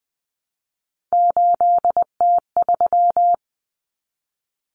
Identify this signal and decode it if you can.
Morse code, heard as a keyed tone, about 20 wpm, 700 Hz. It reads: 8T3
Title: Common Cut Numbers - 3-Digit 20wpm